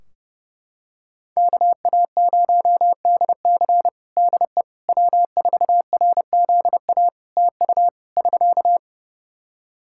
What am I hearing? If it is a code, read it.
Morse code, heard as a keyed tone, about 30 wpm, 700 Hz. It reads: KA0DC DE W4RZA TU <SK>
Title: Practice QSO - Exchange 4 - Ending - 30wpm